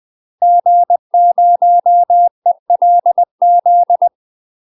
Morse code, heard as a keyed tone, about 20 wpm, 700 Hz. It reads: G0ELZ
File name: UK MASTER QSOs TRAINING / 20WPM / UK MASTER QSOs 0028